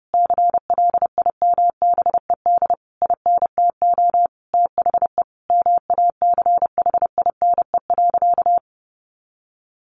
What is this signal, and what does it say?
Morse code, heard as a keyed tone, about 30 wpm, 700 Hz. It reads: CLIMBED INTO THE MACHINE.
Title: A Sound Of Thunder 30wpm